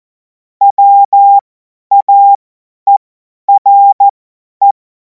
Morse code, heard as a keyed tone, about 14 wpm, spaced as for 10 wpm, 800 Hz. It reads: WAERE
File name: Morsekurs / Witze / Witz 24. Februar 2026 50 F70